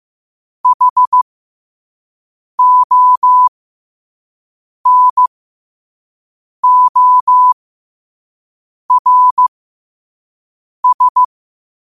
Morse code, heard as a keyed tone, about 15 wpm, spaced as for 5 wpm, 1000 Hz. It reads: HONORS
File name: qotdfiles / 5wpm